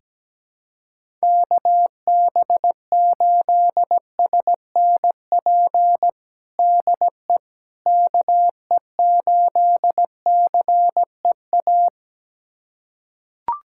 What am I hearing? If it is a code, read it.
Morse code, heard as a keyed tone, about 17 wpm, 700 Hz. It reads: KB8SNP DE KE8CEA
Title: USA Callsign DE Callsign - 6 Characters - 17wpm